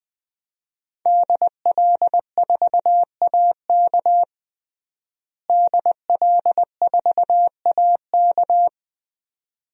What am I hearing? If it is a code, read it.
Morse code, heard as a keyed tone, about 21 wpm, 700 Hz. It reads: DL4AK DL4AK
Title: DXCC-100 Call Signs - 5 Characters - Repeated 2x - 20wpm